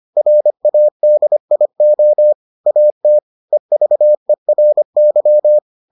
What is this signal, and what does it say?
Morse code, heard as a keyed tone, about 25 wpm, 600 Hz. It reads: RADIO AT EVERY